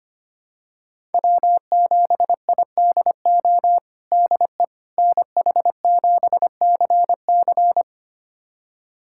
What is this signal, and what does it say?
Morse code, heard as a keyed tone, about 25 wpm, 700 Hz. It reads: W7IDO DE NH7CC
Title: USA Callsign DE Callsign - 5 Characters - 25wpm